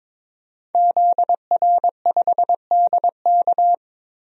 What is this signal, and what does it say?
Morse code, heard as a keyed tone, about 22 wpm, 700 Hz. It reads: ZR5DK
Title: DXCC-100 Call Signs - 5 Characters - Repeated 2x - 22wpm